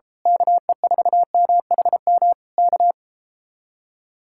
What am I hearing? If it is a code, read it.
Morse code, heard as a keyed tone, about 32 wpm, 700 Hz. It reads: KE4MHM K
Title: Speed-Racing Common QSO Phrases - 28wpm